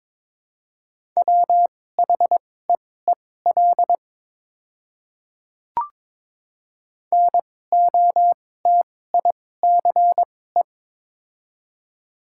Morse code, heard as a keyed tone, about 22 wpm, spaced as for 16 wpm, 690 Hz. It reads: WHEEL NOTICE
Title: 2x Character Spacing - Top 500 Words - 22wpm